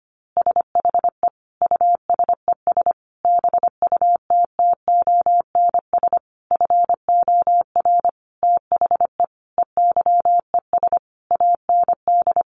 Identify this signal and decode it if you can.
Morse code, heard as a keyed tone, about 25 wpm, 700 Hz. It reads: SHE USES BUTTONS FOR THE EYES AND
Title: Easy Phrases 25wpm